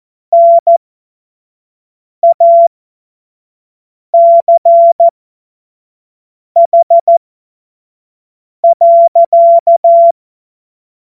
Morse code, heard as a keyed tone, about 14 wpm, spaced as for 5 wpm, 680 Hz. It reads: NACH.
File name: Witz-09-September-2025-25-F70